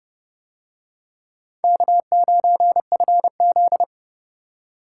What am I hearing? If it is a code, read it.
Morse code, heard as a keyed tone, about 31 wpm, 700 Hz. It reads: K9FZ